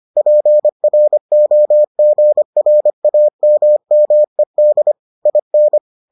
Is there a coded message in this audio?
Yes: Morse code from 0.2 to 5.8 s, heard as a keyed tone, about 25 wpm, 600 Hz. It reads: PROGRAMMED IN